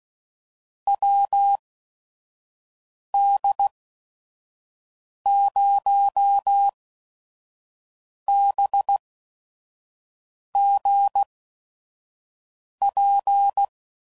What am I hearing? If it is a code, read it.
Morse code, heard as a keyed tone, about 16 wpm, spaced as for 5 wpm, 800 Hz. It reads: WD0BGP
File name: CW Code Groups @5WPM / QSOs / WD0FBT 1